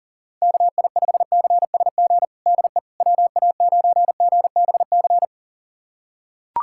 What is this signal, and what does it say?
Morse code, heard as a keyed tone, about 40 wpm, 700 Hz. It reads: KI5CSG DE WA9GBC
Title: USA Callsign DE Callsign - 6 Characters - 40wpm